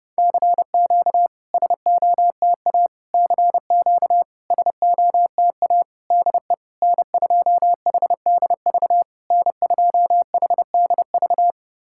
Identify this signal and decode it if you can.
Morse code, heard as a keyed tone, about 32 wpm, 700 Hz. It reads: CQ SOTA CQ SOTA DE N2HDV N2HDV
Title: SOTA QSO Practice 30wpm